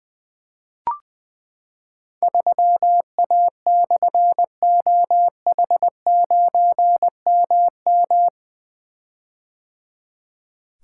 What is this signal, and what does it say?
Morse code, heard as a keyed tone, about 20 wpm, 700 Hz. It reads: 3A/OH9MM
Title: Difficult Call Signs 20wpm